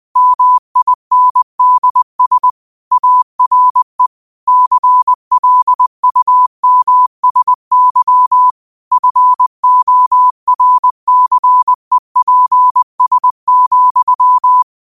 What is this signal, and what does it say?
Morse code, heard as a keyed tone, about 20 wpm, 1000 Hz. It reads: MINDS ARE CLUMSY FORCEPS,
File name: qotdfiles / 20wpm